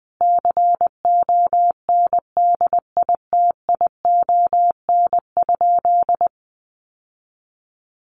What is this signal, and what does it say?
Morse code, heard as a keyed tone, about 20 wpm, 700 Hz. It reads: CONDITION?
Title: Sentences from Top 1300 Words 20wpm